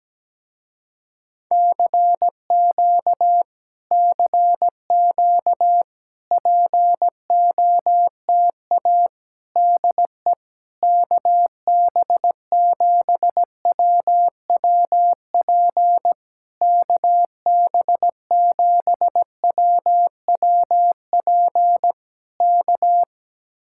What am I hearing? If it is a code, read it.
Morse code, heard as a keyed tone, about 17 wpm, 700 Hz. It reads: CQ CQ POTA DE KB7WWP KB7WWP K
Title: Speed-Racing Common QSO Phrases Encore - 17wpm